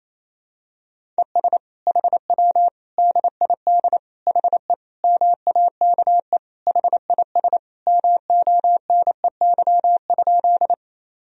Morse code, heard as a keyed tone, about 28 wpm, 700 Hz. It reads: ES HW DID HE MAKE HIS MONEY?
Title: Sentences from Top 400 Words and Abbreviations 28wpm